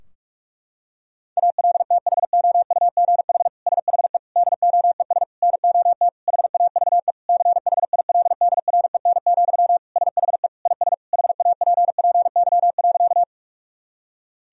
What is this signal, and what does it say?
Morse code, heard as a keyed tone, about 46 wpm, 700 Hz. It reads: ALTHOUGH SHE DOES NOT HAVE CHILDREN, SHE IS HAPPY.
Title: Sentences from Top 1000 Words - Review Part 3 - 45wpm